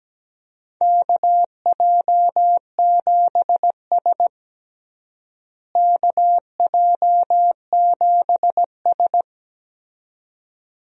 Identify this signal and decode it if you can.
Morse code, heard as a keyed tone, about 17 wpm, 700 Hz. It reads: KJ7S KJ7S
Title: USA Call Signs - 4 Characters - Repeated 2x - 17wpm